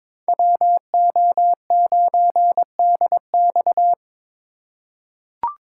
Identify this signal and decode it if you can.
Morse code, heard as a keyed tone, about 22 wpm, 700 Hz. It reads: WO9DX
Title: USA Call Signs - 5 Characters - 4x Character Spacing - 22wpm